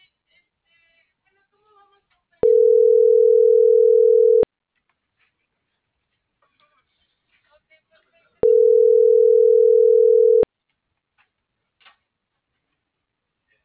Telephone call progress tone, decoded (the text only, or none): ringback tone